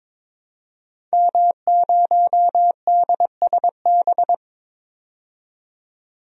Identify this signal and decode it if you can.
Morse code, heard as a keyed tone, about 22 wpm, 700 Hz. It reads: M0DSB
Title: DXCC-100 Call Signs - 5 Characters - 22wpm